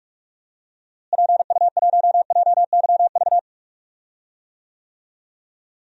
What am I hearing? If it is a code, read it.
Morse code, heard as a keyed tone, about 41 wpm, 700 Hz. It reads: PU1JYV
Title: DXCC-100 Call Signs - 6 Characters - Repeated 2x - 45wpm